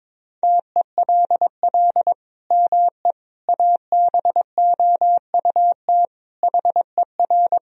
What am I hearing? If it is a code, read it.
Morse code, heard as a keyed tone, about 22 wpm, 700 Hz. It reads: TELL ME ABOUT HER
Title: Sentences from Top 1000 Words - Review Part 2 - 22wpm